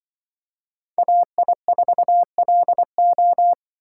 Morse code, heard as a keyed tone, about 26 wpm, 700 Hz. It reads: AI4LO